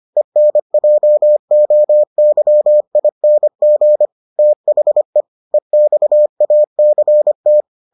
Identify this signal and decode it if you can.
Morse code, heard as a keyed tone, about 25 wpm, 600 Hz. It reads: ENJOYING THE EXACT